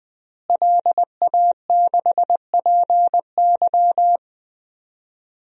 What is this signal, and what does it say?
Morse code, heard as a keyed tone, about 21 wpm, 700 Hz. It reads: LA6PY
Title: DXCC-100 Call Signs - 5 Characters - Repeated 2x - 20wpm